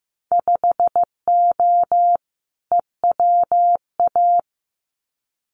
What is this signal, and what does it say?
Morse code, heard as a keyed tone, about 15 wpm, 700 Hz. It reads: 5O EWA
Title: Winter Field Day Exchanges Encore 15wpm